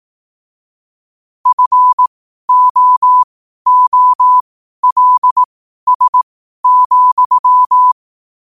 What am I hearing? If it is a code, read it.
Morse code, heard as a keyed tone, about 18 wpm, spaced as for 12 wpm, 1000 Hz. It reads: FOOLS,
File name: qotdfiles / 12wpm